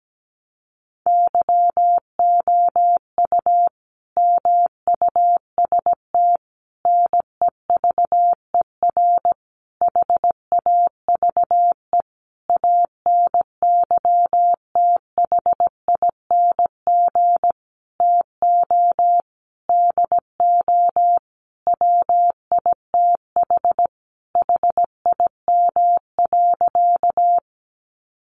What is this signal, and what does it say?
Morse code, heard as a keyed tone, about 17 wpm, 700 Hz. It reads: YOU MUST NEVER HAVE ANYTHING TO DO WITH HIM.
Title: Sentences from Top 1000 Words 17wpm